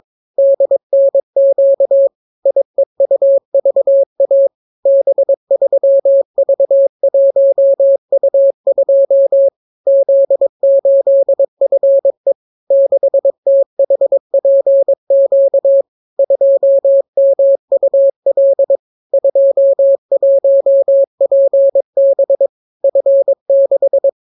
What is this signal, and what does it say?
Morse code, heard as a keyed tone, about 22 wpm, 550 Hz. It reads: DNQ IEUVA B3V1U2 Z8FE 6THPQ 2MUL 21PB F6